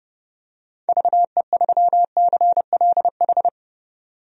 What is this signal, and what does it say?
Morse code, heard as a keyed tone, about 30 wpm, 700 Hz. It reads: VE3CLH
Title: DXCC-100 Call Signs - 6 Characters - 30wpm